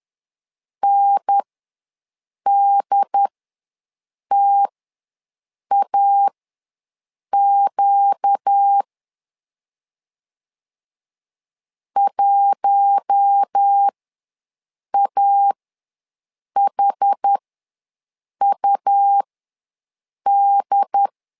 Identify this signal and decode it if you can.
Morse code, heard as a keyed tone, about 10 wpm, spaced as for 6 wpm, 800 Hz. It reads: NDTAQ 1AHUD
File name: LEZIONI / 4 Q1 ws 10wpm 2538